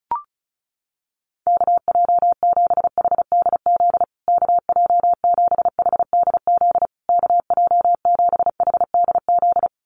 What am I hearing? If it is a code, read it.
Morse code, heard as a keyed tone, about 34 wpm, 700 Hz. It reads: KJ7HDZ KJ7HDZ KJ7HDZ
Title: USA Call Signs Repeated 3x 35wpm